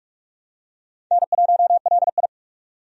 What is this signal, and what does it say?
Morse code, heard as a keyed tone, about 46 wpm, 700 Hz. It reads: N1LI
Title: USA Call Signs - 4 Characters - 45wpm